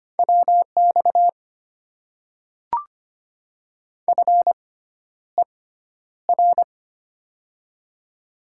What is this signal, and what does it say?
Morse code, heard as a keyed tone, about 25 wpm, spaced as for 11 wpm, 700 Hz. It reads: WX FER
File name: Top 90 QSO Elements - 6x Character Spacing - 25wpm